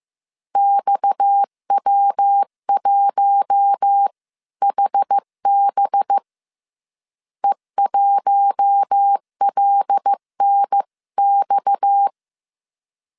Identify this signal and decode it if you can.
Morse code, heard as a keyed tone, about 15 wpm, 800 Hz. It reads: XW1HB E1LNX
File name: CW / DET / 1 6 1500C S C 14wpm 2951